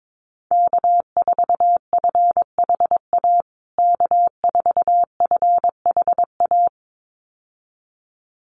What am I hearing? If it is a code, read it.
Morse code, heard as a keyed tone, about 22 wpm, 700 Hz. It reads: K4FHA K4FHA
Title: USA Call Signs Repeated 2x 22wpm